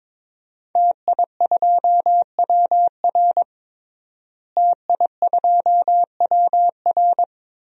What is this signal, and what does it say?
Morse code, heard as a keyed tone, about 22 wpm, 700 Hz. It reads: TI2WR TI2WR
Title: DXCC-100 Call Signs - 5 Characters - Repeated 3x - 22wpm